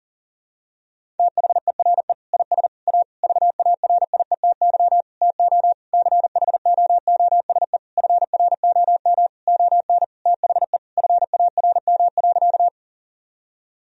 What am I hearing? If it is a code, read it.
Morse code, heard as a keyed tone, about 40 wpm, 700 Hz. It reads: THERE IS A VARIETY TO CHOOSE FROM ON THE FARM.
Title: Sentences from Top 1000 Words - Review Part 4 - 40wpm